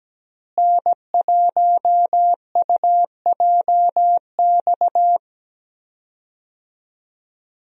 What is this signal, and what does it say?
Morse code, heard as a keyed tone, about 17 wpm, 700 Hz. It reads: N1UJX
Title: USA Call Signs - 5 Characters - Repeated 2x - 17wpm